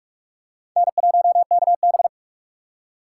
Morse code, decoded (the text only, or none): N1KB